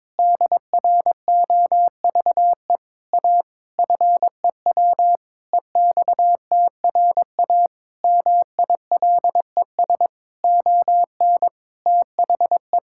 Morse code, heard as a keyed tone, about 21 wpm, 700 Hz. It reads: DROVE A FEW EXTRA MILES ON THE
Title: Pangram Challenge v2 - 22wpm